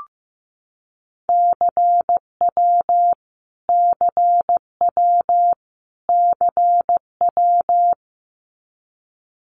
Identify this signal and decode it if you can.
Morse code, heard as a keyed tone, about 15 wpm, 700 Hz. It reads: CW CW CW
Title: Abbreviations Repeated 3x - 15wpm